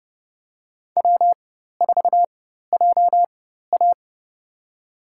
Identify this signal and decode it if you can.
Morse code, heard as a keyed tone, about 30 wpm, spaced as for 14 wpm, 700 Hz. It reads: W4JA
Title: USA Call Signs - 4 Characters - 4x Character Spacing - 30wpm